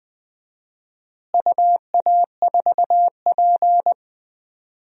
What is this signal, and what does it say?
Morse code, heard as a keyed tone, about 20 wpm, 700 Hz. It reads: UA4P
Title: DXCC-100 Call Signs - 4 Characters - Repeated 2x - 20wpm